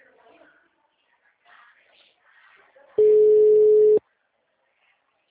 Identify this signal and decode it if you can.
Ringback tone (Europe, 425 Hz).